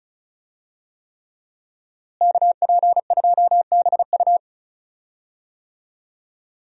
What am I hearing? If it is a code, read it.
Morse code, heard as a keyed tone, about 36 wpm, 700 Hz. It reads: KP2BU
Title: USA Call Signs - 5 Characters - Repeated 3x - 35wpm